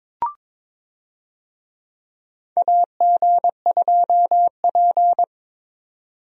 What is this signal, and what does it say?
Morse code, heard as a keyed tone, about 23 wpm, 700 Hz. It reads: AG2P
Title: USA Call Signs - 4 Characters - 22wpm